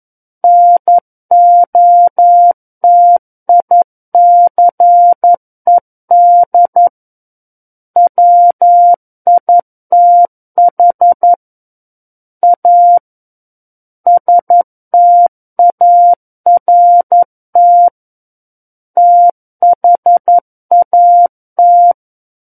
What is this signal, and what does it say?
Morse code, heard as a keyed tone, about 11 wpm, 700 Hz. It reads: NOTICED WITH A START THAT
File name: 11 WPM